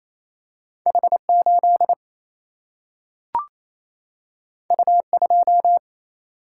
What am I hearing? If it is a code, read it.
Morse code, heard as a keyed tone, about 28 wpm, 700 Hz. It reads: H8 U2
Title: Single-Letter Single-Number 28wpm